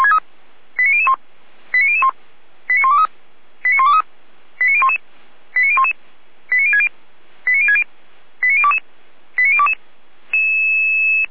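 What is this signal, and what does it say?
Five-tone selective call (ZVEI), heard as quick runs of five tones, tones of about 70 ms. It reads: 89001 89001 89123 89123 89011 89011 89060 89060 89022 89022
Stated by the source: ZVEI five-tone call 89014 received by SDR.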